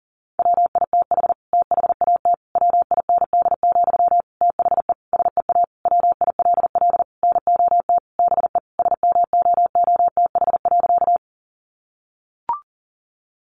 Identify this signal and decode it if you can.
Morse code, heard as a keyed tone, about 40 wpm, 700 Hz. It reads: WITH THAT WIND, THE SEA WILL NOT BE SMOOTH.
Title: Sentences from Top 1400 Words 40wpm